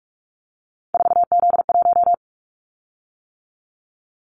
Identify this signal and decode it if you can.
Morse code, heard as a keyed tone, about 46 wpm, 700 Hz. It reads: 4Z1
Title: Callsign Prefixes for Top 50 DXCC - 45wpm